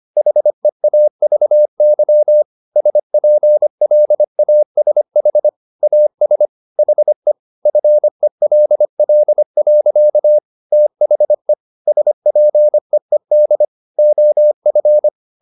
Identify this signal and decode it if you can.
Morse code, heard as a keyed tone, about 25 wpm, 600 Hz. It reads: HEAVY SPLASH AS HE FELL. THE SPEED OF